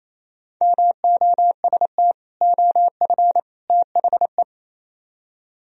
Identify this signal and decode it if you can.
Morse code, heard as a keyed tone, about 35 wpm, 700 Hz. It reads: MOST OF THE
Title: Speed-Racing - Sets of 3 Words 28wpm